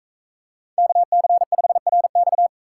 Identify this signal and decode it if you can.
Morse code, heard as a keyed tone, about 44 wpm, 700 Hz. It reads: KC5RX